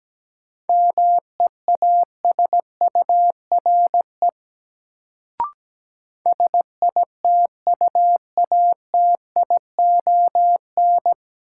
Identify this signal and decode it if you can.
Morse code, heard as a keyed tone, about 17 wpm, 700 Hz. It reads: MEASURE SITUATION
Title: Sentences from Top 1200 Words - New Word Drill Alt - 17wpm